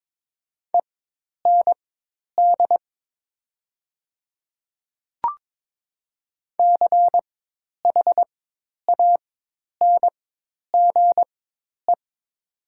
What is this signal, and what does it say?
Morse code, heard as a keyed tone, about 21 wpm, spaced as for 10 wpm, 700 Hz. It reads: END CHANGE
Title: 4x Character Spacing - Top 200 Words - 22wpm